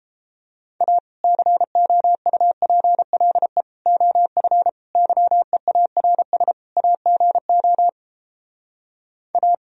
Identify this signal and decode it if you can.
Morse code, heard as a keyed tone, about 29 wpm, 700 Hz. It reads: A COUPLE OF YEARS AGO A